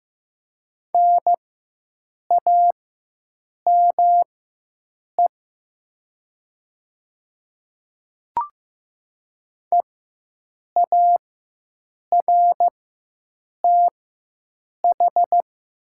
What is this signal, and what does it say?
Morse code, heard as a keyed tone, about 15 wpm, spaced as for 7 wpm, 700 Hz. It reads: NAME EARTH
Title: 4x Character Spacing - Top 200 Words - 15wpm